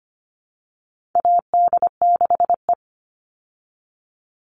Morse code, heard as a keyed tone, about 25 wpm, 700 Hz. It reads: AD6E
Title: Call Signs Repeated 2x - 25wpm